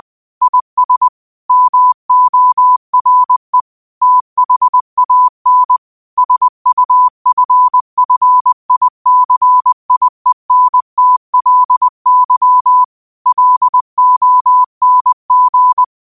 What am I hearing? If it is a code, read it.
Morse code, heard as a keyed tone, about 20 wpm, 1000 Hz. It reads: IS MORE THAN SUFFICIENTLY LONG
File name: qotdfiles / 20wpm